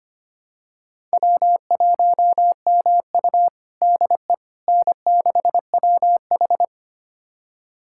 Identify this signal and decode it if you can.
Morse code, heard as a keyed tone, about 25 wpm, 700 Hz. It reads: W1MU DE N6WH
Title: USA Callsign DE Callsign - 4 Characters - 25wpm